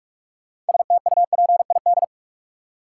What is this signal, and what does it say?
Morse code, heard as a keyed tone, about 46 wpm, 700 Hz. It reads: STUPID
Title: Six Letter Words - 45wpm